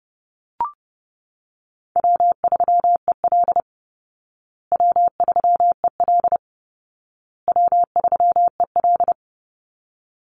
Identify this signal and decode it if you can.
Morse code, heard as a keyed tone, about 30 wpm, 700 Hz. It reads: W3EL W3EL W3EL
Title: Call Signs Repeated 2x - 30wpm